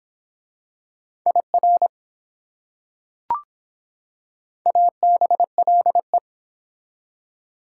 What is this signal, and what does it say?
Morse code, heard as a keyed tone, about 24 wpm, 700 Hz. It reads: IR ABLE